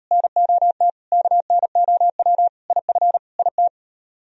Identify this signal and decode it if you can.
Morse code, heard as a keyed tone, about 34 wpm, 700 Hz. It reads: NOT KNOW IF IT